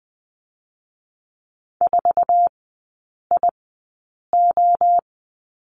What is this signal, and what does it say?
Morse code, heard as a keyed tone, about 20 wpm, spaced as for 8 wpm, 700 Hz. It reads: 4IO
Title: 081 Lesson 14e - Number 2 - Three Characters 20wpm